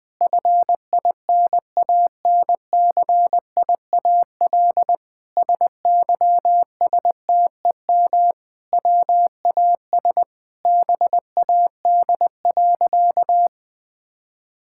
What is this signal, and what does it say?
Morse code, heard as a keyed tone, about 20 wpm, 700 Hz. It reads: FINANCIAL SYSTEM WAS BAD.
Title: Sentences from Top 1000 Words - Review Part 5 - 20wpm